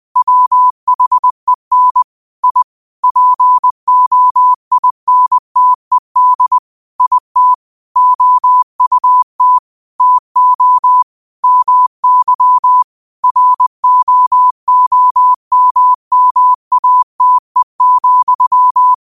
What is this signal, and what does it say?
Morse code, heard as a keyed tone, about 20 wpm, 1000 Hz. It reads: WHEN I POINTED IT OUT TO MY ROOMMATE,
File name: 2025-06-25-20wpm